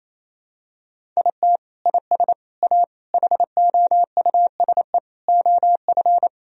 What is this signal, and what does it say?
Morse code, heard as a keyed tone, about 27 wpm, 700 Hz. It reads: IT IS A HOUSE OF
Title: Sentences from Top 1000 Words - Review Part 3 - 28wpm